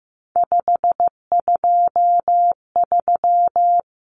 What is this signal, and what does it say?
Morse code, heard as a keyed tone, about 15 wpm, 700 Hz. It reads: 523
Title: Numbers - 3-Digit 15wpm